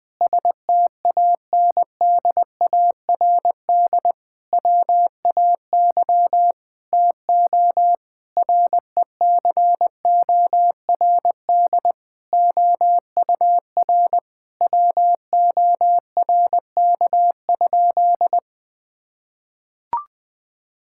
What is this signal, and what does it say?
Morse code, heard as a keyed tone, about 21 wpm, 700 Hz. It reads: STANDARD WAY TO RECORD OUR WORK?
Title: Sentences from Top 1000 Words - Review Part 6 - 20wpm